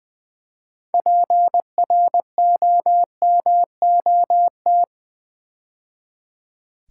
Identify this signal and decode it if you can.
Morse code, heard as a keyed tone, about 20 wpm, 700 Hz. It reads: PROMOT
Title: Build A Word - 3 Letters At A Time - 9-Letter Words - 20wpm